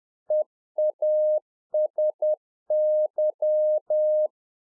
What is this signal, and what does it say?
Morse code, heard as a keyed tone, about 10 wpm, 610 Hz. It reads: EASY